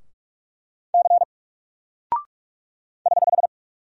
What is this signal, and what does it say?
Morse code, heard as a keyed tone, about 48 wpm, 700 Hz. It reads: C <HH>
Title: Single Character (Letter:Number:Punctuation:Prosign) Alt 45wpm